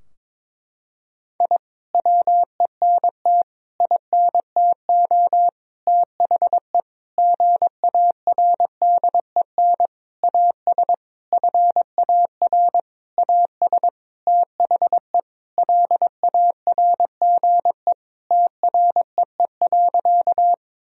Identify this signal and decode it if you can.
Morse code, heard as a keyed tone, about 22 wpm, 700 Hz. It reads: I WENT INTO THE GARDEN AS FAR AS THE LARGE TREE.
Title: Sentences from Top 1000 Words - Review Part 4 - 22wpm